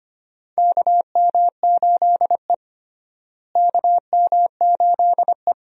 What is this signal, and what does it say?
Morse code, heard as a keyed tone, about 25 wpm, 700 Hz. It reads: KM8E KM8E
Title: USA Call Signs - 4 Characters - Repeated 2x - 25wpm